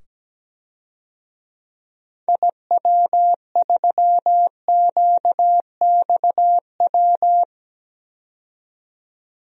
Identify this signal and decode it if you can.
Morse code, heard as a keyed tone, about 17 wpm, 700 Hz. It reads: IW3QXW
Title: DXCC-100 Call Signs - 6 Characters - 17wpm